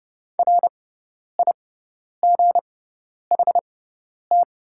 Morse code, heard as a keyed tone, about 30 wpm, spaced as for 10 wpm, 700 Hz. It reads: RIGHT
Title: 6x Character Spacing - Top 300 Words - 30wpm